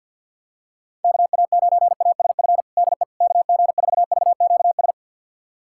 Morse code, heard as a keyed tone, about 51 wpm, 690 Hz. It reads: KA9ASF DE KG4VQS
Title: USA Callsign DE Callsign 50wpm